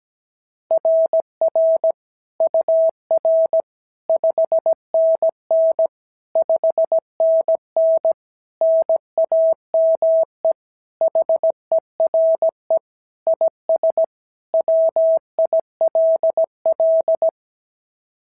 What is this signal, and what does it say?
Morse code, heard as a keyed tone, about 17 wpm, 690 Hz. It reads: RR UR 5NN 5NN NAME HERE IS WILL
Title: SOTA QSO Practice 17wpm